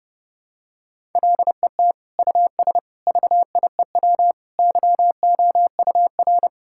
Morse code, heard as a keyed tone, about 31 wpm, 700 Hz. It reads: LET US VIEW YOUR
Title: Sentences from Top 1000 Words - Review Part 6 - 30wpm